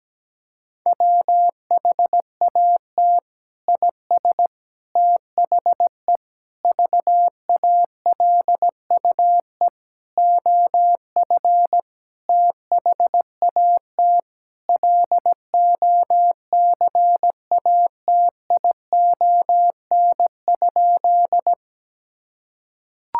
Morse code, heard as a keyed tone, about 17 wpm, 700 Hz. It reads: WHAT IS THE VALUE OF THAT LOCATION?
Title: Sentences from Top 1000 Words - Review Part 6 - 17wpm